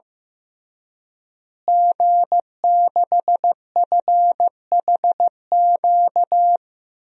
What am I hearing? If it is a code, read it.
Morse code, heard as a keyed tone, about 15 wpm, 700 Hz. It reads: G6FHQ
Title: DXCC-100 Call Signs - 5 Characters - Repeated 3x - 15wpm